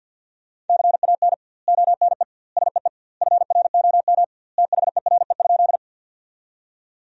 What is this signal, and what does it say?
Morse code, heard as a keyed tone, about 50 wpm, 700 Hz. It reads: CAN ONE SEE FROM THERE?
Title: 2x Word-Spacing - Sentences from Top 100 Words - 50wpm